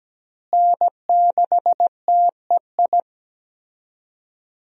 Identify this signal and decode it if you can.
Morse code, heard as a keyed tone, about 17 wpm, 700 Hz. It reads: N6TEI
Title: USA Call Signs - 5 Characters - Repeated 2x - 17wpm